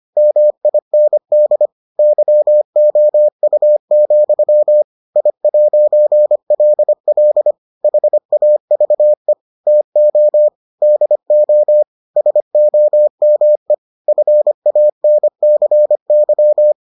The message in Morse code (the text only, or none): MIND YOU, I'LL HAVE TO DO SOME FANCY